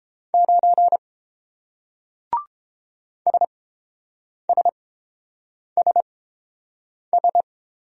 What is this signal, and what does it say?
Morse code, heard as a keyed tone, about 29 wpm, spaced as for 7 wpm, 700 Hz. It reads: 9 SSSS